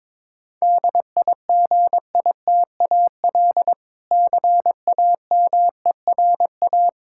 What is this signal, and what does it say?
Morse code, heard as a keyed tone, about 22 wpm, 700 Hz. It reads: DIGITAL CAMERA
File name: Difficult Words with Hints - 22wpm